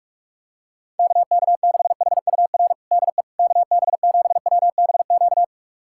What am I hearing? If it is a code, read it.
Morse code, heard as a keyed tone, about 42 wpm, 700 Hz. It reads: KK6HUR DE KB7WBQ
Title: USA Callsign DE Callsign - 6 Characters - 45wpm